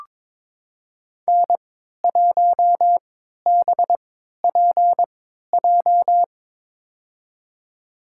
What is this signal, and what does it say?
Morse code, heard as a keyed tone, about 22 wpm, spaced as for 12 wpm, 700 Hz. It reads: N1BPJ
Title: USA Call Signs - 5 Characters - 3x Character Spacing - 22wpm